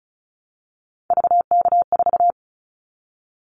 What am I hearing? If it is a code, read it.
Morse code, heard as a keyed tone, about 36 wpm, 700 Hz. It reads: VK4
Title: Callsign Prefixes for Top 50 DXCC - 35wpm